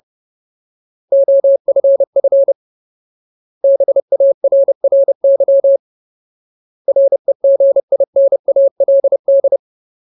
Morse code, heard as a keyed tone, about 30 wpm, 550 Hz. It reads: OFF BARRY REGINALD
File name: Ultimate Word List / 30WPM Ultimate